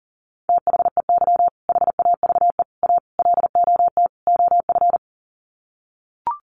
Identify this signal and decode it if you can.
Morse code, heard as a keyed tone, about 40 wpm, 700 Hz. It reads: THEY HAVE A LOT OF
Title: Sets of 5 Words Encore2 40wpm